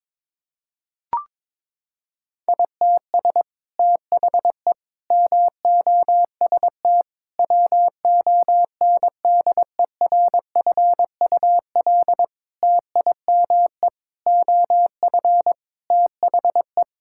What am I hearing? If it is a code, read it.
Morse code, heard as a keyed tone, about 23 wpm, 700 Hz. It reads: ITS THE MOST WONDERFUL TIME OF THE
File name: practice / 22wpm / Sets of 8 Words - 22wpm